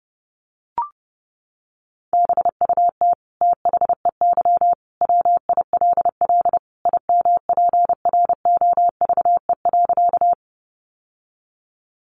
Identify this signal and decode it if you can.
Morse code, heard as a keyed tone, about 30 wpm, 700 Hz. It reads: BUT THEY WILL IMPROVE.
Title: Sentences from Top 800 Words 30wpm